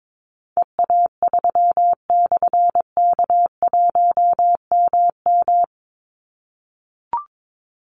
Morse code, heard as a keyed tone, about 22 wpm, 700 Hz. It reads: EA3/K1MM
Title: Call Signs - 22wpm